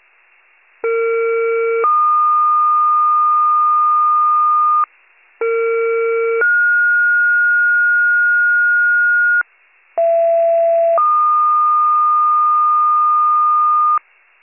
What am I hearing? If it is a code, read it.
Two-tone page, heard as a one-second tone, then a three-second tone, about 460 then 1220 Hz, about 460 then 1430 Hz, about 670 then 1220 Hz. It is heard from 0.8 to 14.0 s.